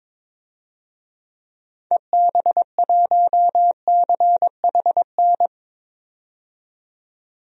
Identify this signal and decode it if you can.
Morse code, heard as a keyed tone, about 22 wpm, 700 Hz. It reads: EB1CHN
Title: DXCC-100 Call Signs - 6 Characters - 22wpm